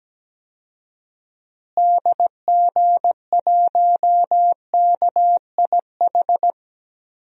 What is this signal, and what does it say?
Morse code, heard as a keyed tone, about 17 wpm, 700 Hz. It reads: DG1KIH
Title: DXCC-100 Call Signs - 6 Characters - Repeated 3x - 17wpm